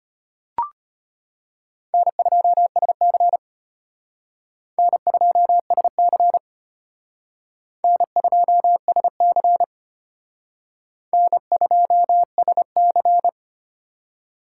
Morse code, heard as a keyed tone, about 32 wpm, 700 Hz. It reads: N2SC N2SC N2SC N2SC